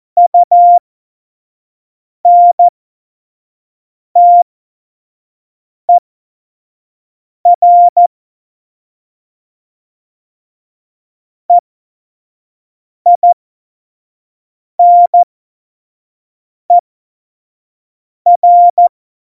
Morse code, heard as a keyed tone, about 14 wpm, spaced as for 5 wpm, 700 Hz. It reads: UNTER EINER
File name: Morsekurs / Witze / Witz 21. Mai 2019, 25 F70